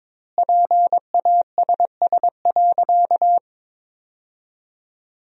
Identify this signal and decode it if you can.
Morse code, heard as a keyed tone, about 22 wpm, 700 Hz. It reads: PASS.
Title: 2x Word-Spacing - Sentences from Top 400 Words - 22wpm